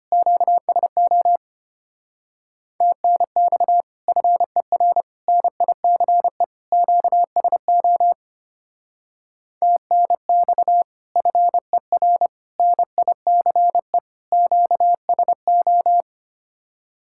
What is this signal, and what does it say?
Morse code, heard as a keyed tone, about 30 wpm, 700 Hz. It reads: QSO TNX FER NICE QSO TNX FER NICE QSO
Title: Speed-Racing Common QSO Phrases - 25wpm